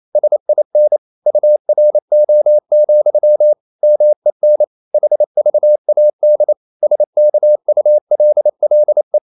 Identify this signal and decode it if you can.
Morse code, heard as a keyed tone, about 28 wpm, 600 Hz. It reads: SIN URO, MEN HVAD SKULLE
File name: Ferie QRQ 28wpm